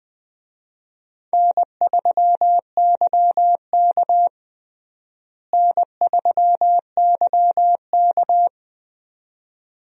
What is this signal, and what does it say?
Morse code, heard as a keyed tone, about 20 wpm, 700 Hz. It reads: N3YK N3YK
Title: USA Call Signs - 4 Characters - Repeated 2x - 20wpm